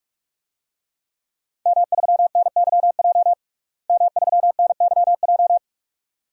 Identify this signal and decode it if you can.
Morse code, heard as a keyed tone, about 46 wpm, 700 Hz. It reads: M3NYJ M3NYJ
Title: DXCC-100 Call Signs - 5 Characters - Repeated 2x - 45wpm